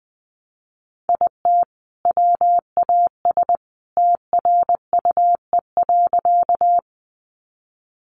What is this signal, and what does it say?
Morse code, heard as a keyed tone, about 20 wpm, 700 Hz. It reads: IT WAS TRUE.
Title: Sentences from Top 400 Words 20wpm